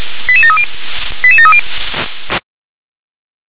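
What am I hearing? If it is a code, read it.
Five-tone selective call (ZVEI), heard as quick runs of five tones, tones of about 70 ms. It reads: 88730 88730
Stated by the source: ZVEI five-tone call 88154 received by SDR.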